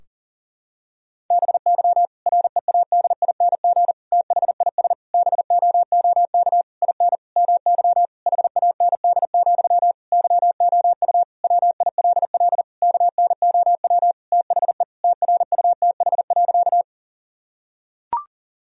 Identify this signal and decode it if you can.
Morse code, heard as a keyed tone, about 37 wpm, 700 Hz. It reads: BY READING THIS BOOK IN MY HAND, YOU WILL KNOW THE TRUTH.
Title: Sentences from Top 1000 Words - Review Part 3 - 40wpm